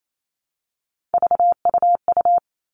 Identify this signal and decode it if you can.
Morse code, heard as a keyed tone, about 28 wpm, 700 Hz. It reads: VUU
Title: All Cut Numbers - 3-Digit 28wpm